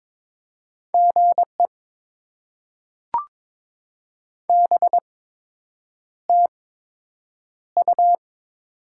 Morse code, heard as a keyed tone, about 22 wpm, spaced as for 8 wpm, 700 Hz. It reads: GE BTU
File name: Top 20 QSO Elements - 8x Character Spacing - 22wpm